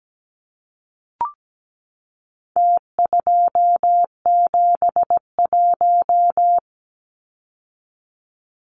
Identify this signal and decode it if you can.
Morse code, heard as a keyed tone, about 17 wpm, 700 Hz. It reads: T271
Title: Numbers - Contest Style - 4-Digit 17wpm